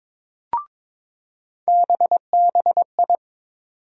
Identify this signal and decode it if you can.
Morse code, heard as a keyed tone, about 23 wpm, 700 Hz. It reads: BBI
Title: USA Call Sign Postfixes - 3 Letters 22wpm